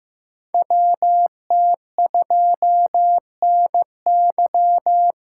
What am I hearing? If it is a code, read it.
Morse code, heard as a keyed tone, about 15 wpm, 700 Hz. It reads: WT2NY
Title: USA Call Signs - 3x Character Spacing - 15wpm